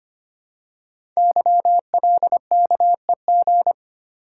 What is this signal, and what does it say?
Morse code, heard as a keyed tone, about 25 wpm, 700 Hz. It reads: YLKEG
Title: Code Groups - 5 Letters - 25wpm